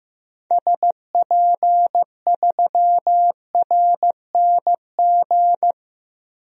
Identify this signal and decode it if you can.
Morse code, heard as a keyed tone, about 16 wpm, 700 Hz. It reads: SP3RNG
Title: DXCC-100 Call Signs - 6 Characters - Repeated 3x - 15wpm